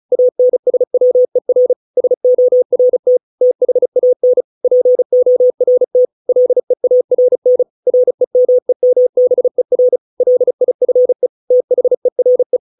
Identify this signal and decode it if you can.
Morse code, heard as a keyed tone, about 34 wpm, 500 Hz. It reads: ANSWER SORT THAN PORT LEARN REMEMBER LIFE THERE